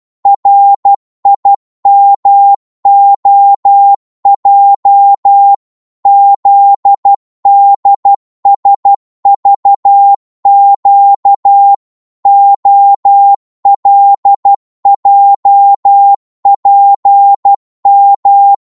Morse code, heard as a keyed tone, about 12 wpm, 800 Hz. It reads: RIMOJ ZDSVQ OLJPM
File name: Morse / ltr-12wpm